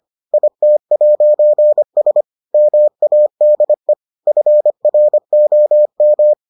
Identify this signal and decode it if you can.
Morse code, heard as a keyed tone, about 25 wpm, 600 Hz. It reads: IT'S MADE FROM